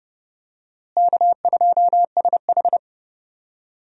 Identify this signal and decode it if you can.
Morse code, heard as a keyed tone, about 30 wpm, 700 Hz. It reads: K2SH
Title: USA Call Signs - 4 Characters - Repeated 3x - 30wpm